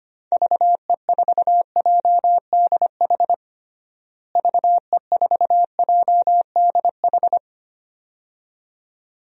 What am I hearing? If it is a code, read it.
Morse code, heard as a keyed tone, about 25 wpm, 700 Hz. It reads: VE4JDH VE4JDH
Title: DXCC-100 Call Signs - 6 Characters - Repeated 2x - 25wpm